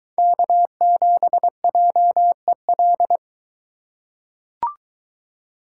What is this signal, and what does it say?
Morse code, heard as a keyed tone, about 22 wpm, 700 Hz. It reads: K7JEL